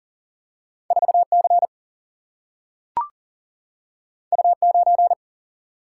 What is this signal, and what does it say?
Morse code, heard as a keyed tone, about 40 wpm, 700 Hz. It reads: 4C U9
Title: Single-Letter Single-Number 40wpm